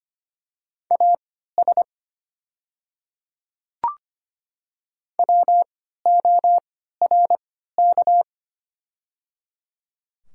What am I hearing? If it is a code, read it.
Morse code, heard as a keyed tone, about 25 wpm, spaced as for 14 wpm, 690 Hz. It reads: AS WORK
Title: 3x Character Spacing - Top 200 Words - 25wpm